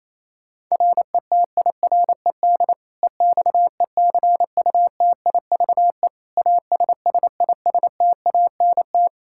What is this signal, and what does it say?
Morse code, heard as a keyed tone, about 28 wpm, 700 Hz. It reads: RETIRED EXECUTIVE ASSISTANT
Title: Practice QSO - Exchange 3 - More Info - 28wpm